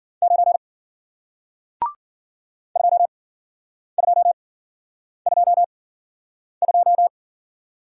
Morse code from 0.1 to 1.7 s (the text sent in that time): 9